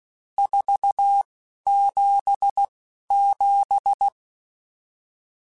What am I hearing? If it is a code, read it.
Morse code, heard as a keyed tone, about 16 wpm, spaced as for 11 wpm, 800 Hz. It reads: 477
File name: RSTs / RST-mp3s / Various RST @10